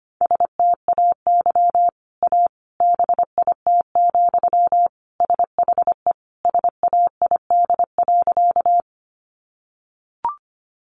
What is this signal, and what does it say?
Morse code, heard as a keyed tone, about 25 wpm, 700 Hz. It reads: STAY A BIT, SHE SAID.
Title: Sentences from Top 900 Words 25wpm